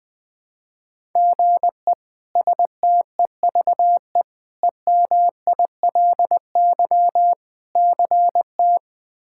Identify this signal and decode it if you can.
Morse code, heard as a keyed tone, about 21 wpm, 700 Hz. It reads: GE STEVE EMILY CT
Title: K1USN SST QSO Practice - 20wpm